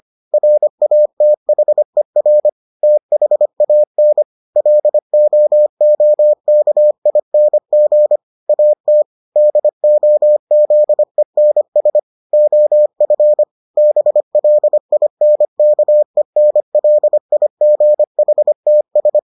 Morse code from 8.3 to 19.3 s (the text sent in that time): AT DOZENS OF BLINKENLIGHTS